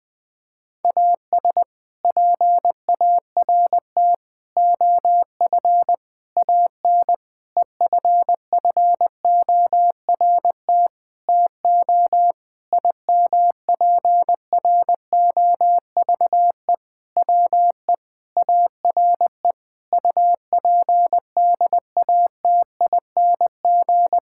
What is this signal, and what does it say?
Morse code, heard as a keyed tone, about 20 wpm, 700 Hz. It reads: AS PART OF AN EFFORT TO IMPROVE WE ARE UPDATING
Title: Sentences with 6-Word Phrases - Spoken Beginning - 20wpm